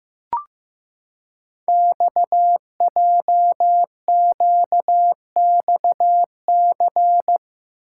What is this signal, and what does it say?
Morse code, heard as a keyed tone, about 15 wpm, 700 Hz. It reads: XJQXC
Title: Code Groups - 5 Letters - 15wpm